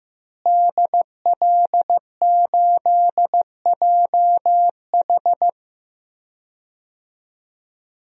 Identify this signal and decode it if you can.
Morse code, heard as a keyed tone, about 15 wpm, 700 Hz. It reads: DL8JH
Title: DXCC-100 Call Signs - 5 Characters - Repeated 3x - 15wpm